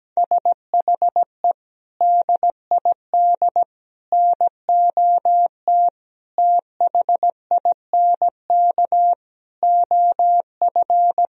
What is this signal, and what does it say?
Morse code, heard as a keyed tone, about 17 wpm, 700 Hz. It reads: SHE DID NOT THINK OF
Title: Sentences from Top 1000 Words - Review Part 3 - 17wpm